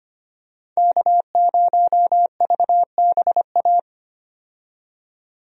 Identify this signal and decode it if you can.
Morse code, heard as a keyed tone, about 25 wpm, 690 Hz. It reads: K0VBA
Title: USA Call Signs - 5 Characters - 25wpm